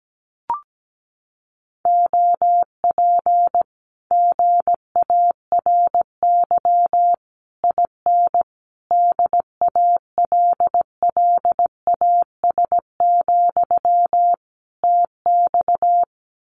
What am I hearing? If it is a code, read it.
Morse code, heard as a keyed tone, about 17 wpm, 700 Hz. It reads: OP GARY IN DALLAS, TX
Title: Common QSO Phrases Encore 17wpm